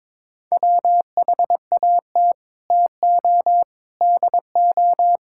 Morse code, heard as a keyed tone, about 22 wpm, 700 Hz. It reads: WHAT TO DO
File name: practice / 22wpm / Sets of 8 Words - 22wpm